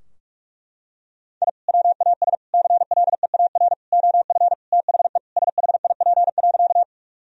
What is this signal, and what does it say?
Morse code, heard as a keyed tone, about 41 wpm, 700 Hz. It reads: I WAS CLEAR OF THE SHIP.
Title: Sentences from Top 1000 Words - Review Part 2 - 45wpm